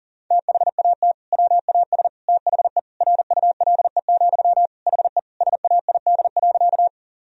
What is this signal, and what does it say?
Morse code, heard as a keyed tone, about 40 wpm, 700 Hz. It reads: THAT WAS THE RULE, HE SAID.
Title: Sentences from Top 1000 Words - Review Part 2 - 40wpm